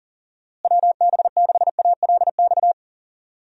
Morse code, heard as a keyed tone, about 37 wpm, 700 Hz. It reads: WB6ALX
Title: USA Call Signs - 6 Characters - Repeated 3x - 40wpm